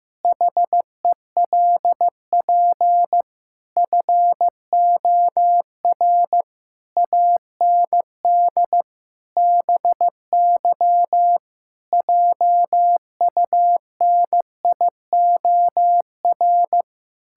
Morse code, heard as a keyed tone, about 16 wpm, 700 Hz. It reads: HELP FOR AND BY JUNIOR
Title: Pangram Challenge v2 - 15wpm